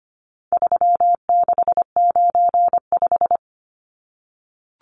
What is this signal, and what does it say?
Morse code, heard as a keyed tone, about 25 wpm, 700 Hz. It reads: 3695
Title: Numbers - 4-Digit 25wpm